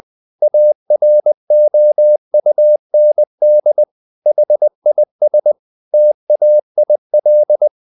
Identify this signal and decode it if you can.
Morse code, heard as a keyed tone, about 20 wpm, 600 Hz. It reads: AROUND HIS TAIL
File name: the old man and the sea part 2 20wpm